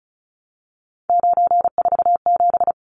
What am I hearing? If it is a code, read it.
Morse code, heard as a keyed tone, about 34 wpm, 700 Hz. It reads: 947
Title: Numbers - 3-Digit 35wpm